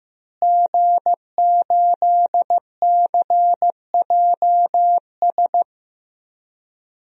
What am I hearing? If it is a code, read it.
Morse code, heard as a keyed tone, about 15 wpm, 700 Hz. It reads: G8CJS
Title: DXCC-100 Call Signs - 5 Characters - Repeated 3x - 15wpm